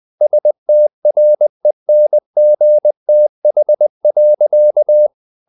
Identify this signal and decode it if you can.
Morse code, heard as a keyed tone, about 20 wpm, 600 Hz. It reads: STRENGTH.
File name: the old man and the sea / the old man and the sea part 1 20wpm